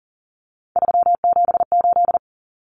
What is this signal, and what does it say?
Morse code, heard as a keyed tone, about 39 wpm, 700 Hz. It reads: 378
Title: Common Cut Numbers - 3-Digit 40wpm